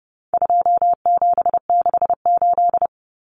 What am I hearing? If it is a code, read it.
Morse code, heard as a keyed tone, about 29 wpm, 700 Hz. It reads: 2768
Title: Common Cut Numbers - 4-Digit 30wpm